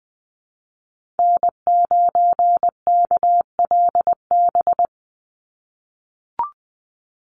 Morse code, heard as a keyed tone, about 20 wpm, 700 Hz. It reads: N9KLB
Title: USA Call Signs 20wpm